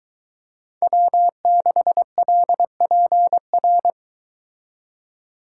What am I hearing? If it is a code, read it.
Morse code, heard as a keyed tone, about 23 wpm, 700 Hz. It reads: W6LPR